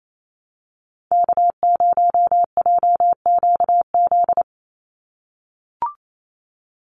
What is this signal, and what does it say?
Morse code, heard as a keyed tone, about 28 wpm, 700 Hz. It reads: K0JQZ
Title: NA SOTA Activator Callsigns 28wpm